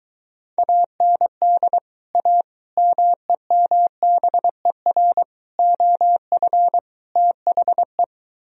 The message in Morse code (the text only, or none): AND A MEMBER OF THE